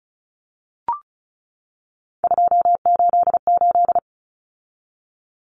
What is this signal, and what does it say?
Morse code, heard as a keyed tone, about 34 wpm, 700 Hz. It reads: 288
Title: Common Cut Numbers - 3-Digit 35wpm